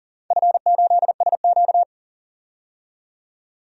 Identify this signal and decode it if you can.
Morse code, heard as a keyed tone, about 38 wpm, 700 Hz. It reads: F8SQ
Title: DXCC-100 Call Signs - 4 Characters - 40wpm